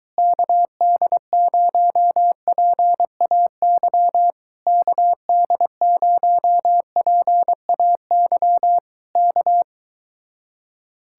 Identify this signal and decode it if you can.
Morse code, heard as a keyed tone, about 22 wpm, 700 Hz. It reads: KD0PAY KD0PAY K